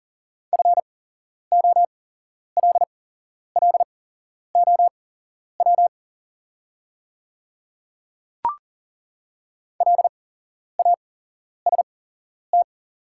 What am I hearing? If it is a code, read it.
Morse code, heard as a keyed tone, about 40 wpm, spaced as for 11 wpm, 690 Hz. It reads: FOLLOW LAST
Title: 8x Character Spacing - Top 300 Words - 40wpm